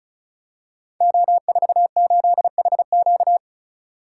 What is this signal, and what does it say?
Morse code, heard as a keyed tone, about 36 wpm, 700 Hz. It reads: O48HQ
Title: Code Groups - 5 Letter:Numbers - 35wpm